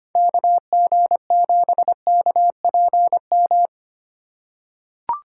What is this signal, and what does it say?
Morse code, heard as a keyed tone, about 25 wpm, 690 Hz. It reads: KG7KPM
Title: USA Call Signs - 6 Characters - 2x Character Spacing - 25wpm